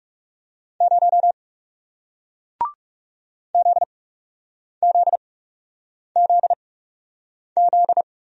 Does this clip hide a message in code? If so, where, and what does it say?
Yes: Morse code from 0.8 to 8.0 s, heard as a keyed tone, about 40 wpm, spaced as for 8 wpm, 700 Hz. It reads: 0 ZZZZ